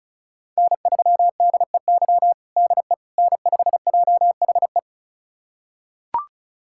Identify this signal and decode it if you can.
Morse code, heard as a keyed tone, about 36 wpm, 700 Hz. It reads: N3DEY DE N5JHE
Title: USA Callsign DE Callsign - 5 Characters - 35wpm